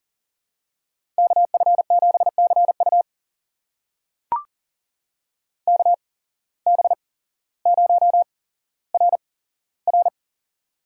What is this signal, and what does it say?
Morse code, heard as a keyed tone, about 39 wpm, spaced as for 17 wpm, 700 Hz. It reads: KF7CU KB0RR
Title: USA Call Signs - 5 Characters - 8x Character Spacing - 40wpm